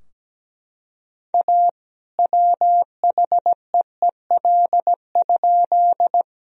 Morse code, heard as a keyed tone, about 17 wpm, 700 Hz. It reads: A WHEEL?
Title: Sentences from Top 1000 Words - Review Part 3 - 17wpm